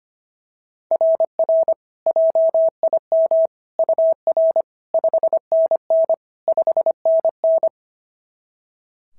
Morse code, heard as a keyed tone, about 25 wpm, 690 Hz. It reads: RR JIM UR 5NN 5NN
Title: SOTA QSO Practice 25wpm